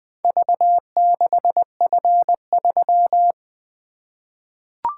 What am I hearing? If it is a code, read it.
Morse code, heard as a keyed tone, about 20 wpm, 700 Hz. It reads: V6F3
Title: Four Letter-Number Combinations - 20wpm